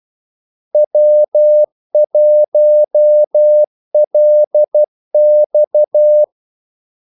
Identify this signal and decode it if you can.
Morse code, heard as a keyed tone, about 12 wpm, 600 Hz. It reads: W1LX
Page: 2 ekstra ordmellemrum mellem kaldesignalerne